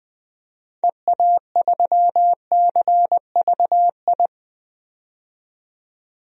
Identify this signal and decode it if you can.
Morse code, heard as a keyed tone, about 20 wpm, 700 Hz. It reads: EA3CVI
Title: DXCC-100 Call Signs - 6 Characters - 20wpm